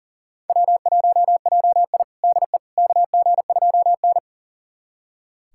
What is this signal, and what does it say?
Morse code, heard as a keyed tone, about 40 wpm, 700 Hz. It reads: W1JI DE KG2N
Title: USA Callsign DE Callsign - 4 Characters - 40wpm